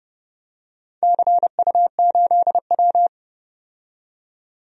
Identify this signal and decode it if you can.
Morse code, heard as a keyed tone, about 29 wpm, 700 Hz. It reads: CU8W
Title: DXCC-100 Call Signs - 4 Characters - 30wpm